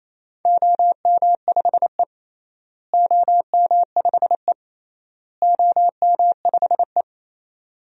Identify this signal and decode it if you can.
Morse code, heard as a keyed tone, about 28 wpm, 700 Hz. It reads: OM5E OM5E OM5E
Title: DXCC-100 Call Signs - 4 Characters - Repeated 3x - 28wpm